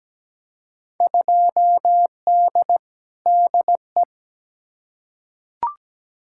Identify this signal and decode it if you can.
Morse code, heard as a keyed tone, about 17 wpm, 700 Hz. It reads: 2D DE
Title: Field Day Exchanges Encore - 17wpm